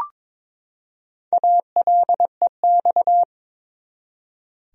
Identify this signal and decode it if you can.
Morse code, heard as a keyed tone, about 21 wpm, 700 Hz. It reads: ALEX
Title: USA Common Names - 22wpm